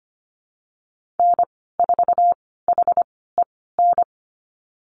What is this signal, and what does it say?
Morse code, heard as a keyed tone, about 25 wpm, spaced as for 16 wpm, 690 Hz. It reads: N4HEN
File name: Lesson 15g - Number 5 - Call Signs 25wpm